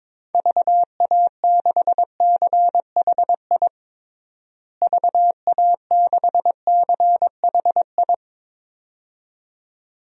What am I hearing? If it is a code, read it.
Morse code, heard as a keyed tone, about 21 wpm, 700 Hz. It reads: VA6CHI VA6CHI
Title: DXCC-100 Call Signs - 6 Characters - Repeated 2x - 22wpm